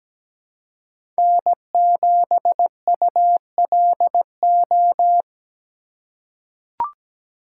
Morse code, heard as a keyed tone, about 17 wpm, 700 Hz. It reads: N7ULO
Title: USA Call Signs - 5 Characters - 2x Character Spacing - 17wpm